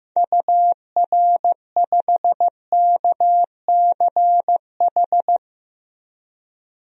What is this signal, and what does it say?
Morse code, heard as a keyed tone, about 15 wpm, 700 Hz. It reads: UR5KCH
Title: DXCC-100 Call Signs - 6 Characters - 15wpm